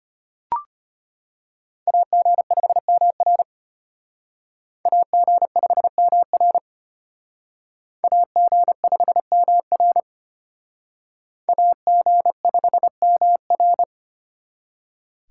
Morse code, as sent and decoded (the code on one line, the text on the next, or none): .- --. ..... -- .-. / .- --. ..... -- .-. / .- --. ..... -- .-. / .- --. ..... -- .-.
AG5MR AG5MR AG5MR AG5MR